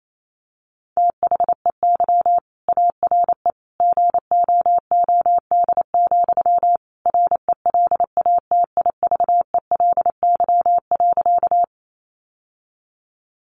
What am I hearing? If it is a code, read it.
Morse code, heard as a keyed tone, about 28 wpm, 700 Hz. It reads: THEY ARE GOOD, RELATIVELY.
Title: Sentences from Top 1100 Words 28wpm